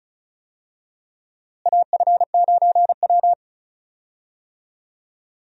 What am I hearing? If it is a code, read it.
Morse code, heard as a keyed tone, about 36 wpm, 700 Hz. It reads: AF9W
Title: USA Call Signs - 4 Characters - Repeated 3x - 35wpm